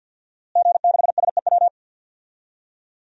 Morse code, heard as a keyed tone, about 53 wpm, 700 Hz. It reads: G6SEW
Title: DXCC-100 Call Signs - 5 Characters - Repeated 2x - 50wpm